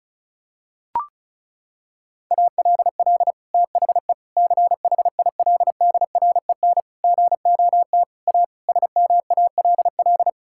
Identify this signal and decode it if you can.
Morse code, heard as a keyed tone, about 36 wpm, 700 Hz. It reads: ALL THE CHILDREN GOT A SMALL
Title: Sentences from Top 1000 Words - Review Part 6 - 35wpm